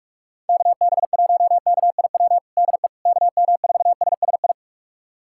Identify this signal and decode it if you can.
Morse code, heard as a keyed tone, about 46 wpm, 700 Hz. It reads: KB1KIW DE KM4SSI
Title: USA Callsign DE Callsign 45wpm